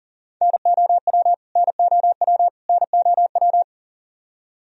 Morse code, heard as a keyed tone, about 40 wpm, 700 Hz. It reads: NOW NOW NOW
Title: Top 100 Words Repeated 40wpm